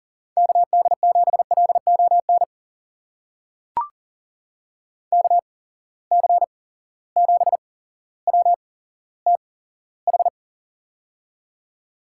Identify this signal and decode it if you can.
Morse code, heard as a keyed tone, about 40 wpm, spaced as for 17 wpm, 700 Hz. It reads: KD7LON KC7WTH
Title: USA Call Signs - 6 Characters - 8x Character Spacing - 40wpm